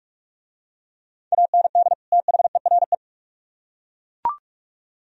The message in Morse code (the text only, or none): AND THERE